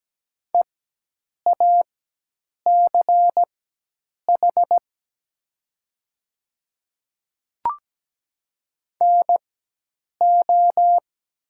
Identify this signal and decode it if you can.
Morse code, heard as a keyed tone, about 17 wpm, spaced as for 8 wpm, 700 Hz. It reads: EACH NO
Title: 4x Character Spacing - Top 100 Words - 17wpm